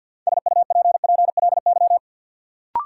Morse code, heard as a keyed tone, about 53 wpm, 700 Hz. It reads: SUPPLY
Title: Sentences from Top 1200 Words - New Word Drill Alt - 50wpm